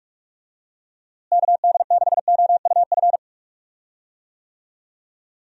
Morse code, heard as a keyed tone, about 43 wpm, 700 Hz. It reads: KD6OUF
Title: USA Call Signs - 6 Characters - Repeated 3x - 45wpm